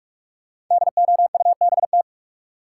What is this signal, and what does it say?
Morse code, heard as a keyed tone, about 46 wpm, 700 Hz. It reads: DOUBT
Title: Sentences from Top 1500 Words - New Word Drill - 45wpm